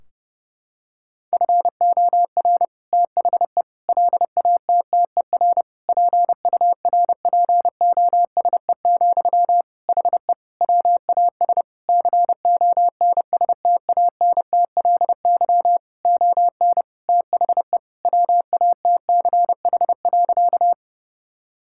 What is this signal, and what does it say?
Morse code, heard as a keyed tone, about 30 wpm, 700 Hz. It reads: FOR THE LATTER PURPOSE, HE WAS CONSTANTLY ON THE WATCH.
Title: Sentences from Top 1600 Words 30wpm